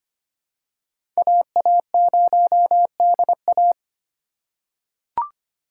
Morse code, heard as a keyed tone, about 25 wpm, 700 Hz. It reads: AA0DA
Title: USA Call Signs - 5 Characters - 3x Character Spacing - 25wpm